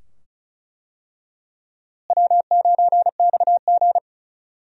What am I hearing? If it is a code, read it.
Morse code, heard as a keyed tone, about 34 wpm, 700 Hz. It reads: W9XG
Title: USA Call Signs - 4 Characters - Repeated 2x - 35wpm